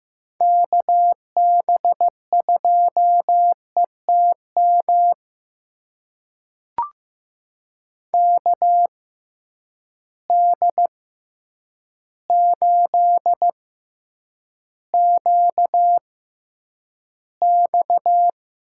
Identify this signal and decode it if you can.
Morse code, heard as a keyed tone, about 16 wpm, spaced as for 8 wpm, 700 Hz. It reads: KB2ETM KD8QX
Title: USA Call Signs - 6 Characters - 6x Character Spacing - 15wpm